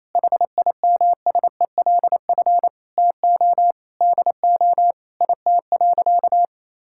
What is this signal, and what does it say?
Morse code, heard as a keyed tone, about 28 wpm, 700 Hz. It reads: HIMSELF TO DO IT.
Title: Sentences from Top 1600 Words 28wpm